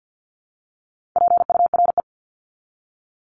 Morse code, heard as a keyed tone, about 50 wpm, 700 Hz. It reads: PURE
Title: Top 1251-1500 Words 50wpm